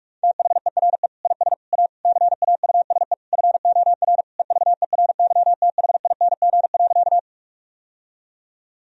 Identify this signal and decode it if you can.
Morse code, heard as a keyed tone, about 46 wpm, 700 Hz. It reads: THERE IS A CAUSE FOR EVERYTHING.
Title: Sentences from Top 1000 Words - Review Part 1 - 45wpm